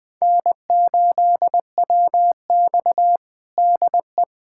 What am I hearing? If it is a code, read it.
Morse code, heard as a keyed tone, about 20 wpm, 700 Hz. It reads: N8WX DE
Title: USA Callsign DE Callsign - 4 Characters - 20wpm